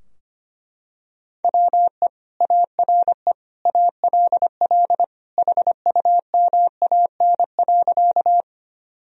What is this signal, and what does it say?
Morse code, heard as a keyed tone, about 25 wpm, 700 Hz. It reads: WE ARE ALL HUMAN.
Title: Sentences from Top 1000 Words - Review Part 3 - 25wpm